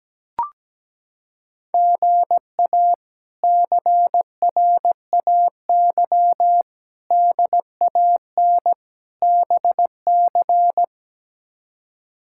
Morse code, heard as a keyed tone, about 17 wpm, 700 Hz. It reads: GA CRAY DAN BC
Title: K1USN SST QSO Practice - 17wpm